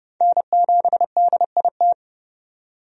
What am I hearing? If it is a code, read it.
Morse code, heard as a keyed tone, about 30 wpm, 700 Hz. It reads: N7DIT
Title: USA Call Signs - 5 Characters - 30wpm